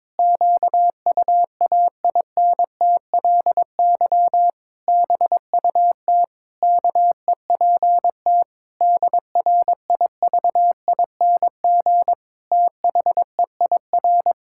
Morse code, heard as a keyed tone, about 22 wpm, 700 Hz. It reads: QUAINTLY BUT KEPT DRIVING THEIR
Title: Pangram Challenge v2 - 22wpm